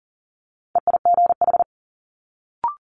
Morse code, heard as a keyed tone, about 42 wpm, 700 Hz. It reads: EIGH
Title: Multi-letter Phonemes Quick Fire 40wpm